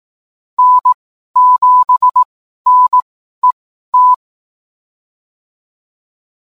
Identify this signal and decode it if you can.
Morse code, heard as a keyed tone, about 18 wpm, spaced as for 12 wpm, 1000 Hz. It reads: N7NET